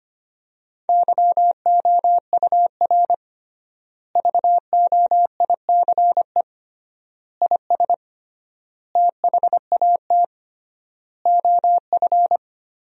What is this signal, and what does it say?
Morse code, heard as a keyed tone, about 25 wpm, 700 Hz. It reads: YOUR VOICE IS THAT OF
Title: 3x Word-Spacing - Sentences from Top 500 Words - 25wpm